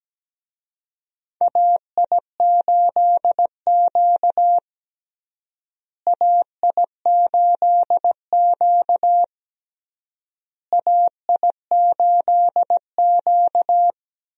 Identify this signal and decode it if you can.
Morse code, heard as a keyed tone, about 17 wpm, 700 Hz. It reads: AI8Q AI8Q AI8Q
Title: USA Call Signs - 4 Characters - Repeated 3x - 17wpm